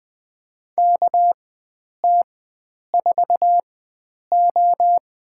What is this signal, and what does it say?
Morse code, heard as a keyed tone, about 20 wpm, spaced as for 9 wpm, 700 Hz. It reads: KT4O
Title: USA Call Signs - 4x Character Spacing - 20wpm